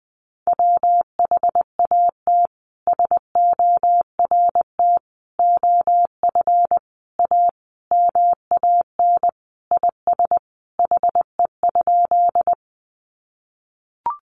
Morse code, heard as a keyed tone, about 19 wpm, 700 Hz. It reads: WHAT SORT OF A MAN IS HE?
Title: Sentences from Top 1300 Words 20wpm